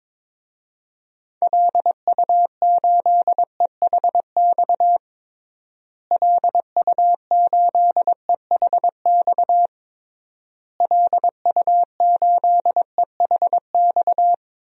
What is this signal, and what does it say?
Morse code, heard as a keyed tone, about 22 wpm, 700 Hz. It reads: LU8EHX LU8EHX LU8EHX
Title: DXCC-100 Call Signs - 6 Characters - Repeated 3x - 22wpm